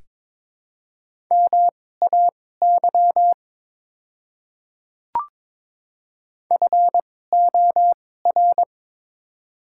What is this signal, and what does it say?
Morse code, heard as a keyed tone, about 22 wpm, spaced as for 16 wpm, 700 Hz. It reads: MAY FOR
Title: 2x Character Spacing - Top 300 Words - 22wpm